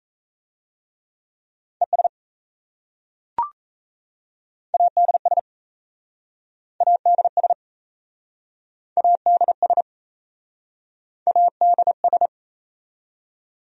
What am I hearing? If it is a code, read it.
Morse code, heard as a keyed tone, about 37 wpm, 700 Hz. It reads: ES ADS ADS ADS ADS